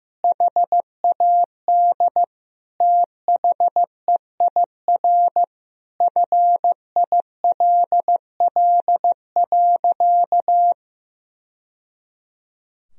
Morse code, heard as a keyed tone, about 15 wpm, 700 Hz. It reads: HAD THEIR FILL.
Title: Sentences from Top 1000 Words - Review Part 3 - 15wpm